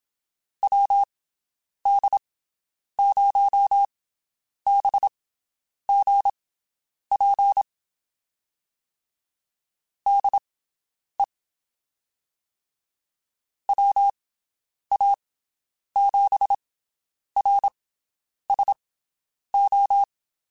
Morse code, heard as a keyed tone, about 27 wpm, spaced as for 9 wpm, 800 Hz. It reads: WD0BGP DE WA7RSO
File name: (spaced) / QSOs Spaced-mp3s / WD0BGP 1 @25 sp